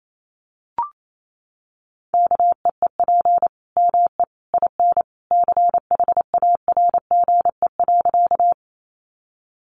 Morse code, heard as a keyed tone, about 28 wpm, 700 Hz. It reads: KEEP ME IN CHARGE.
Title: Sentences from Top 900 Words 28wpm